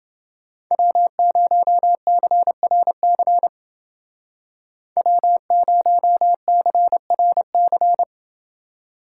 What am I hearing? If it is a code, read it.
Morse code, heard as a keyed tone, about 28 wpm, 700 Hz. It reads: W0CRC W0CRC